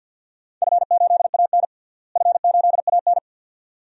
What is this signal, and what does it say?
Morse code, heard as a keyed tone, about 46 wpm, 700 Hz. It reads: F8AN F8AN
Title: DXCC-100 Call Signs - 4 Characters - Repeated 2x - 50wpm